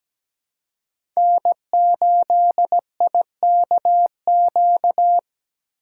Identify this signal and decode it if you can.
Morse code, heard as a keyed tone, about 20 wpm, 700 Hz. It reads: N8IKQ
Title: Speed Racing - USA 1x3 Call Signs 17wpm